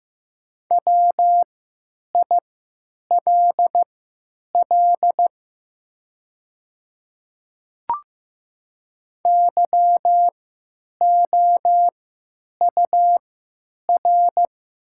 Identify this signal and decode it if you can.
Morse code, heard as a keyed tone, about 15 wpm, spaced as for 9 wpm, 700 Hz. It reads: WILL YOUR
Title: 3x Character Spacing - Top 200 Words - 15wpm